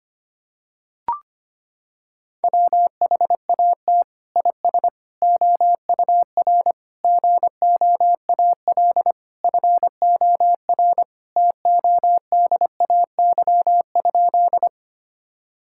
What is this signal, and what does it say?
Morse code, heard as a keyed tone, about 25 wpm, 700 Hz. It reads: WHAT IS OUR GOAL FOR TODAY?
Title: Sentences from Top 1000 Words - Review Part 5 - 25wpm